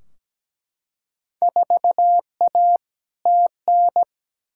Morse code, heard as a keyed tone, about 17 wpm, 700 Hz. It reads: 4A TN
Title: Field Day Exchanges - 17wpm